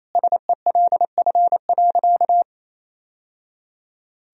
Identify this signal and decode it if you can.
Morse code, heard as a keyed tone, about 28 wpm, 700 Hz. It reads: SELF.
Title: Sentences from Top 1000 Words - Review Part 1 - 28wpm